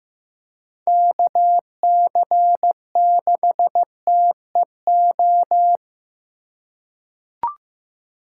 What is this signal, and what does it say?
Morse code, heard as a keyed tone, about 15 wpm, 700 Hz. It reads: KC6TEO
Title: USA Call Signs - 6 Characters - 2x Character Spacing - 15wpm